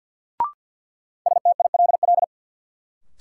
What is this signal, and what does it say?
Morse code, heard as a keyed tone, about 52 wpm, 700 Hz. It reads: STILL
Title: Sentences from Top 300 Words - New Word Drill - 50wpm